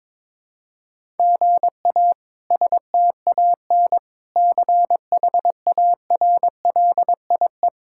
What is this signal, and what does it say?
Morse code, heard as a keyed tone, about 22 wpm, 700 Hz. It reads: GA STAN CHARLIE
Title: K1USN SST QSO Practice - 22wpm